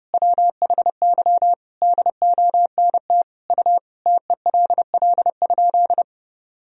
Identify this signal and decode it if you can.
Morse code, heard as a keyed tone, about 28 wpm, 700 Hz. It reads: WHY DONT U TELL?
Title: Sentences from Top 300 Words and Abbreviations 30wpm